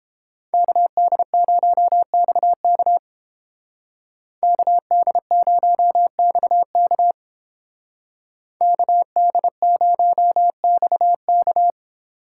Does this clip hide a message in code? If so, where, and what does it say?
Yes: Morse code from 0.5 to 11.7 s, heard as a keyed tone, about 30 wpm, 700 Hz. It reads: KD0XK KD0XK KD0XK